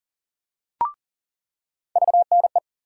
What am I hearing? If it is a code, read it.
Morse code, heard as a keyed tone, about 40 wpm, 700 Hz. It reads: VNE
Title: USA Call Sign Postfixes - 3 Letters 40wpm